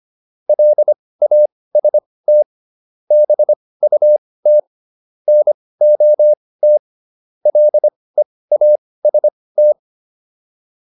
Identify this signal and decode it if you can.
Morse code, heard as a keyed tone, about 25 wpm, spaced as for 18 wpm, 600 Hz. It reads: LAST BUT NOT LEAST